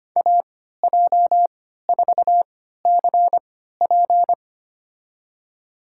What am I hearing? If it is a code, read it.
Morse code, heard as a keyed tone, about 25 wpm, spaced as for 14 wpm, 700 Hz. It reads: AJ4CP
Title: USA Call Signs - 5 Characters - 3x Character Spacing - 25wpm